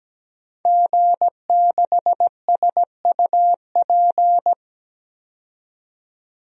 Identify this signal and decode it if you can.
Morse code, heard as a keyed tone, about 17 wpm, 700 Hz. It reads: G6SUP
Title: DXCC-100 Call Signs - 5 Characters - 17wpm